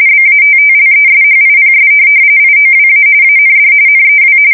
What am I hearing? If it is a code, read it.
RTTY, about 45.45 baud. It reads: BATTLESHIPS IOWA AND NEW